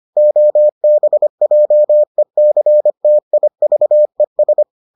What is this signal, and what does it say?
Morse code, heard as a keyed tone, about 25 wpm, 600 Hz. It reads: OBJECTIVES